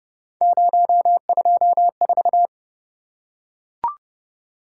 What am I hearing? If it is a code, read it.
Morse code, heard as a keyed tone, about 30 wpm, 700 Hz. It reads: 024